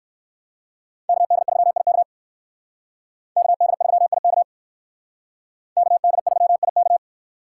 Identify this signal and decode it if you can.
Morse code, heard as a keyed tone, about 57 wpm, 700 Hz. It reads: KD3IK KD3IK KD3IK